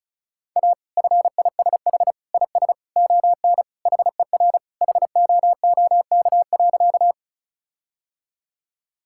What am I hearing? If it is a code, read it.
Morse code, heard as a keyed tone, about 34 wpm, 700 Hz. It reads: A FISH IS ON HER HOOK.
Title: Sentences from Top 1600 Words 35wpm